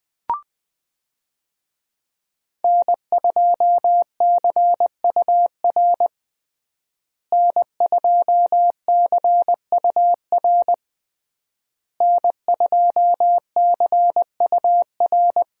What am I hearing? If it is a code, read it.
Morse code, heard as a keyed tone, about 20 wpm, 700 Hz. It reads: N2CUR N2CUR N2CUR
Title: USA Call Signs - 5 Characters - Repeated 3x - 20wpm